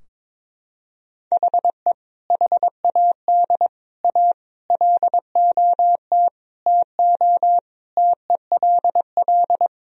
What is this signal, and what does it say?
Morse code, heard as a keyed tone, about 22 wpm, 700 Hz. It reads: HE HAD A LOT TO TELL
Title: Sentences from Top 1000 Words - Review Part 3 - 22wpm